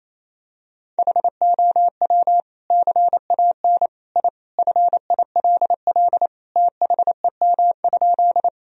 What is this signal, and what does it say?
Morse code, heard as a keyed tone, about 28 wpm, 700 Hz. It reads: HOW CAN I FILL THEM?
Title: Sentences from Top 1000 Words - Review Part 3 - 28wpm